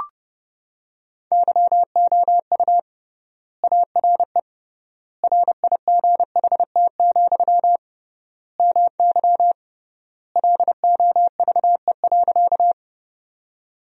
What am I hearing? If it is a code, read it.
Morse code, heard as a keyed tone, about 30 wpm, 700 Hz. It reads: YOU ARE RIGHT, MY LOVE.
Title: 3x Word-Spacing - Sentences from Top 400 Words - 30wpm